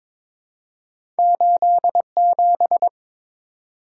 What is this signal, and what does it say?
Morse code, heard as a keyed tone, about 22 wpm, 700 Hz. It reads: 87
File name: practice / 22wpm / Two Letter-Number Combinations - 22wpm